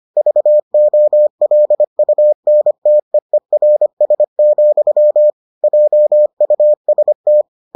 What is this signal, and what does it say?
Morse code, heard as a keyed tone, about 25 wpm, 600 Hz. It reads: VOLUNTEERS, JUST